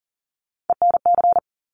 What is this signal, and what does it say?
Morse code, heard as a keyed tone, about 40 wpm, 700 Hz. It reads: ENC
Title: Top Three Letter Combinations in Words 40wpm